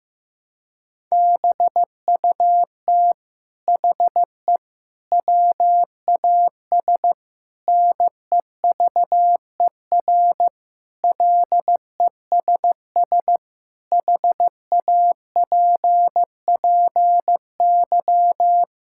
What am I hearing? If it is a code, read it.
Morse code, heard as a keyed tone, about 15 wpm, 700 Hz. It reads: BUT HE WAS NEVER LESS HAPPY
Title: Sentences from Top 1000 Words - Review Part 5 - 15wpm